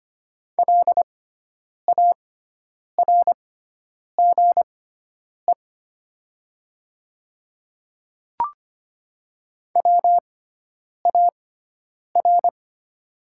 Morse code, heard as a keyed tone, about 25 wpm, spaced as for 9 wpm, 700 Hz. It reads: LARGE WAR
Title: 6x Character Spacing - Top 500 Words - 25wpm